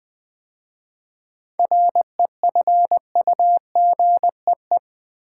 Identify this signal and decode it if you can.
Morse code, heard as a keyed tone, about 20 wpm, 700 Hz. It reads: REFUGEE
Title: 7 Letter Words - First 3 Letters Given - 20wpm